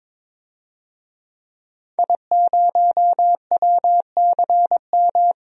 Morse code, heard as a keyed tone, about 21 wpm, 700 Hz. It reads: I0WCM
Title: DXCC-100 Call Signs - 5 Characters - 22wpm